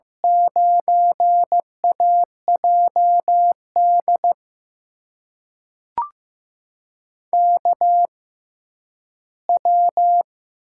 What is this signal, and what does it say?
Morse code, heard as a keyed tone, about 15 wpm, spaced as for 10 wpm, 700 Hz. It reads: 9AJD KW
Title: USA Call Signs - 6x Character Spacing - 15wpm